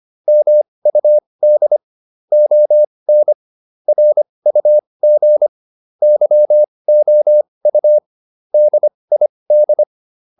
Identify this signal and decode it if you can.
Morse code, heard as a keyed tone, about 25 wpm, spaced as for 20 wpm, 600 Hz. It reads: MUD ON RUG YOU DID